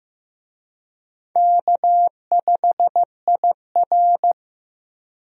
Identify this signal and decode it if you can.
Morse code, heard as a keyed tone, about 15 wpm, 700 Hz. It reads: K5IR
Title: USA Call Signs - 4 Characters - 15wpm